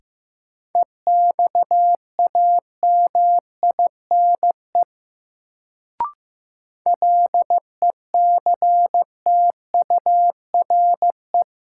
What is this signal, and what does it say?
Morse code, heard as a keyed tone, about 16 wpm, 700 Hz. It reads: EXAMINE LECTURE
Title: Sentences from Top 1600 Words - New Word Drill - 15wpm